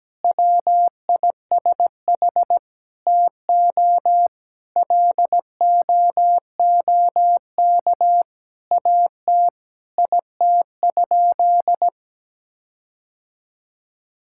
Morse code, heard as a keyed tone, about 17 wpm, 700 Hz. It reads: WISH TO LOOK AT IT?
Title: Sentences from Top 1600 Words 17wpm